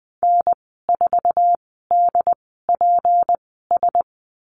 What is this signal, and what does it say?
Morse code, heard as a keyed tone, about 20 wpm, spaced as for 15 wpm, 700 Hz. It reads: N4DPS
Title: 161 Lesson 26g - Letter g - Call Signs 20wpm